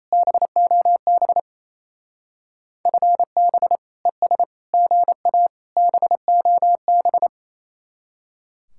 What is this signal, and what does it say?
Morse code, heard as a keyed tone, about 37 wpm, 700 Hz. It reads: BOB FB ES GA BOB